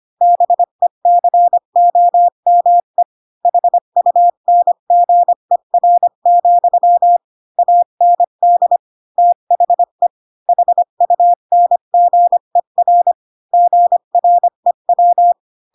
Morse code, heard as a keyed tone, about 25 wpm, 700 Hz. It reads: BECOME HUNGER, AND THE HUNGER GREW
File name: Pinocchio / Pinocchio ch5 20 25 ik1hge